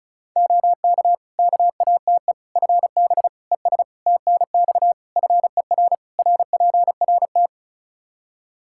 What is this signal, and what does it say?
Morse code, heard as a keyed tone, about 36 wpm, 700 Hz. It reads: OK KATE FB ES TNX FER RPRT
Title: Practice QSO - Exchange 2 - Info - 35wpm